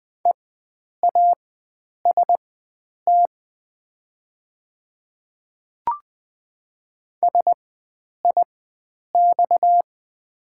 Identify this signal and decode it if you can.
Morse code, heard as a keyed tone, about 20 wpm, spaced as for 9 wpm, 700 Hz. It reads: EAST SIX
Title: 4x Character Spacing - Top 500 Words - 20wpm